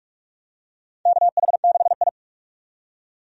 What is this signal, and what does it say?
Morse code, heard as a keyed tone, about 44 wpm, 700 Hz. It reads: KH6I
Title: DXCC-100 Call Signs - 4 Characters - 45wpm